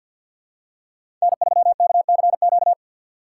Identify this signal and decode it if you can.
Morse code, heard as a keyed tone, about 50 wpm, 700 Hz. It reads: N3KCQ
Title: USA Call Signs - 5 Characters - 4x Character Spacing - 50wpm